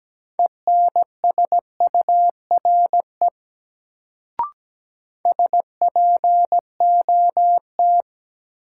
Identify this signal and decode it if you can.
Morse code, heard as a keyed tone, about 17 wpm, 700 Hz. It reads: ENSURE SPOT
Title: Sentences from Top 1100 Words - New Word Drill - 17wpm